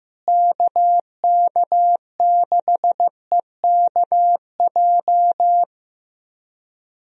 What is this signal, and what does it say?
Morse code, heard as a keyed tone, about 15 wpm, 700 Hz. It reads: KK6EKJ
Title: USA Call Signs - 6 Characters - Repeated 3x - 15wpm